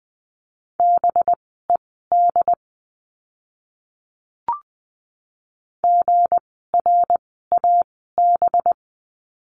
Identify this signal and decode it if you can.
Morse code, heard as a keyed tone, about 20 wpm, spaced as for 14 wpm, 700 Hz. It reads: BED GRAB
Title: 185 Lesson 30f - Letter b - Words 20wpm